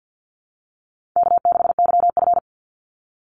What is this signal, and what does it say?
Morse code, heard as a keyed tone, about 48 wpm, 700 Hz. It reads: K6YF
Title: Easy Call Signs Repeated 3x - 50wpm